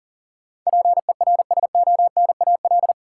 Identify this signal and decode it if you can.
Morse code, heard as a keyed tone, about 40 wpm, 700 Hz. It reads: PERSONAL
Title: Sentences from Top 700 Words - New Word Drill - 40wpm